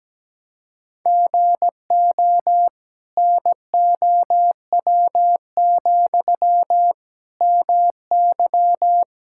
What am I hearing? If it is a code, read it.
Morse code, heard as a keyed tone, about 17 wpm, 700 Hz. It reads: GO NOW, MY
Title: Sentences from Top 1000 Words - Review Part 5 - 17wpm